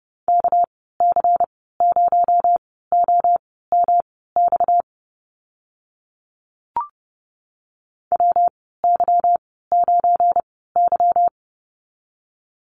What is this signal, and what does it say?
Morse code, heard as a keyed tone, about 31 wpm, spaced as for 17 wpm, 700 Hz. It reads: KC0OMX WY9Y
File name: Lesson 37g - Letter x - Call Signs 30wpm